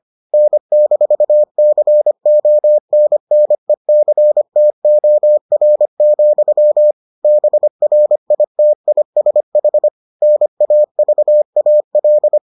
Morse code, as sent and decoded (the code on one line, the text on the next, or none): -. -....- -.-. --- -. -. -.-. - --- .-. --..-- / -... .-. .. - .. ... .... / -. .- ...- .- .-..
N-CONNECTOR, BRITISH NAVAL